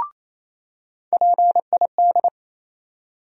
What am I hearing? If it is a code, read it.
Morse code, heard as a keyed tone, about 28 wpm, 700 Hz. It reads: PID
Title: USA Call Sign Postfixes - 3 Letters 28wpm